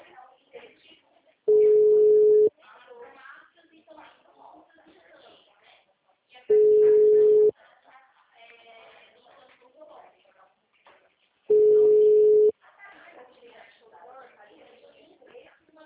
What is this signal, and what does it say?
Ringback tone (Europe, 425 Hz), with speech.